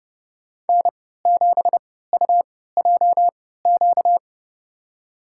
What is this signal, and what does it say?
Morse code, heard as a keyed tone, about 30 wpm, spaced as for 17 wpm, 690 Hz. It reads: N7UJQ
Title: USA Call Signs - 3x Character Spacing - 30wpm